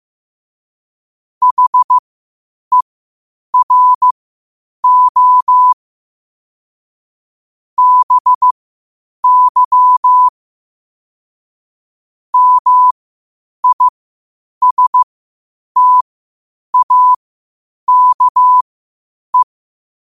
Morse code, heard as a keyed tone, about 15 wpm, spaced as for 8 wpm, 1000 Hz. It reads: HERO BY MISTAKE
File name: qotdfiles / 8wpm